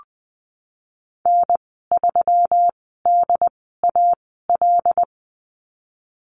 Morse code, heard as a keyed tone, about 20 wpm, spaced as for 15 wpm, 700 Hz. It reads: N3DAL
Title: 154 Lesson 25g - Letter p - Call Signs 20wpm